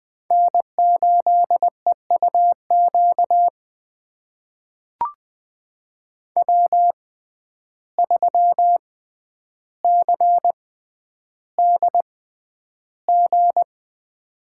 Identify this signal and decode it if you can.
Morse code, heard as a keyed tone, about 20 wpm, spaced as for 10 wpm, 700 Hz. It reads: N8EUQ W3CDG
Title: USA Call Signs - 5 Characters - 6x Character Spacing - 20wpm